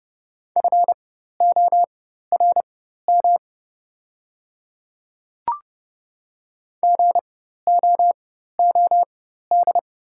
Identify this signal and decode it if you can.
Morse code, heard as a keyed tone, about 30 wpm, spaced as for 14 wpm, 700 Hz. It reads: FORM GOOD
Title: 4x Character Spacing - Top 200 Words - 30wpm